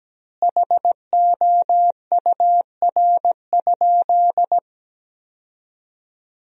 Morse code, heard as a keyed tone, about 17 wpm, 700 Hz. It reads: HOUR?
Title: 4x Word-Spacing - Sentences from Top 400 Words - 17wpm